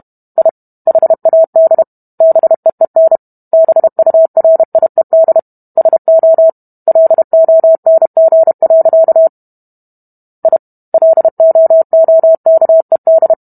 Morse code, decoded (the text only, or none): I HAD BEEN BURIED SO LONG. I LOOKED